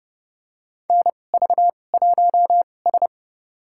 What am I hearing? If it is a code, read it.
Morse code, heard as a keyed tone, about 31 wpm, spaced as for 22 wpm, 700 Hz. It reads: NV1S
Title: USA Call Signs - 4 Characters - 2x Character Spacing - 30wpm